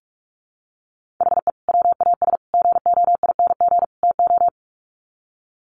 Morse code, heard as a keyed tone, about 44 wpm, 700 Hz. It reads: HE WAS GOING TO
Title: Sets of 4 Words 45wpm